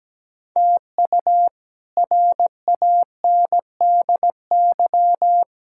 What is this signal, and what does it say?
Morse code, heard as a keyed tone, about 17 wpm, 700 Hz. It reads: TU RANDY
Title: K1USN SST QSO Practice - 17wpm